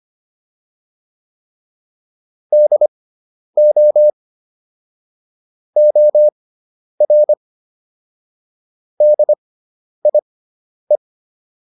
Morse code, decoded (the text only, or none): DO OR DIE